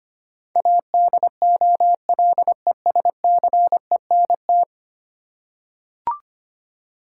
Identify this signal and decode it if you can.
Morse code, heard as a keyed tone, about 25 wpm, 700 Hz. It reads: ADOLESCENT
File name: Ten Letter Words - 25wpm